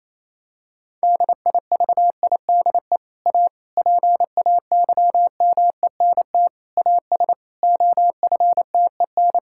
Morse code, heard as a keyed tone, about 27 wpm, 700 Hz. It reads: DIVIDE A PAYMENT AS OFTEN
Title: Sentences from Top 1600 Words 28wpm